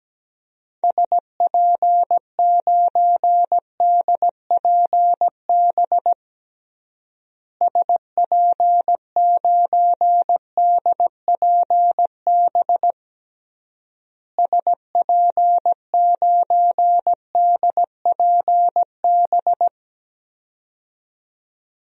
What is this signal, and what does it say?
Morse code, heard as a keyed tone, about 17 wpm, 700 Hz. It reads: SP9DPB SP9DPB SP9DPB
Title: DXCC-100 Call Signs - 6 Characters - Repeated 3x - 17wpm